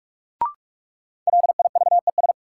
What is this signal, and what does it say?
Morse code, heard as a keyed tone, about 43 wpm, 700 Hz. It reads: LIVES
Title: Sets of 5 Words Encore - New Word Drill - 45wpm